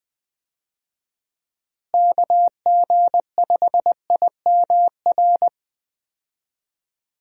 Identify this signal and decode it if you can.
Morse code, heard as a keyed tone, about 21 wpm, 700 Hz. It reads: KG5IMR
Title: USA Call Signs - 6 Characters - Repeated 2x - 20wpm